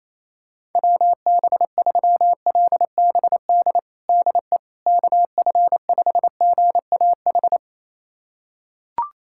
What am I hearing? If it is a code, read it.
Morse code, heard as a keyed tone, about 28 wpm, 700 Hz. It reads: WB3LBD DE KF5GAH
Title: USA Callsign DE Callsign 28wpm